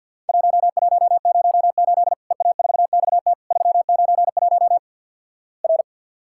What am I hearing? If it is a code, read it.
Morse code, heard as a keyed tone, about 53 wpm, 700 Hz. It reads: 1108 EA4XT 391 R
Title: Worked All Europe DX Contest - QTC Exchange - 50wpm